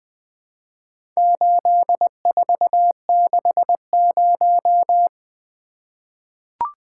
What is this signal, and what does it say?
Morse code, heard as a keyed tone, about 20 wpm, 700 Hz. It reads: 8460
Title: Four Letter-Number Combinations - 20wpm